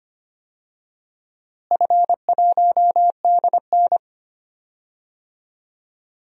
Morse code, heard as a keyed tone, about 25 wpm, 700 Hz. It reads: F1DN
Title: DXCC-100 Call Signs - 4 Characters - Repeated 3x - 25wpm